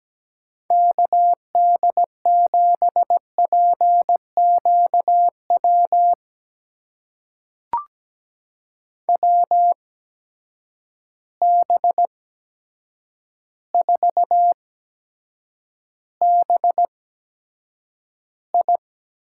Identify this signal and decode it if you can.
Morse code, heard as a keyed tone, about 17 wpm, spaced as for 8 wpm, 700 Hz. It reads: KD7PQW WB4BI
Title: USA Call Signs - 6 Characters - 8x Character Spacing - 17wpm